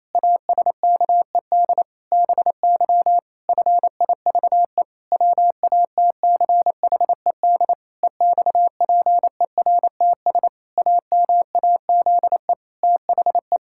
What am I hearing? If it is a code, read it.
Morse code, heard as a keyed tone, about 27 wpm, 700 Hz. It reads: ASKED BY FIVE WATCHED EXPERTS AMAZE THE
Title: Pangram Challenge v2 - 28wpm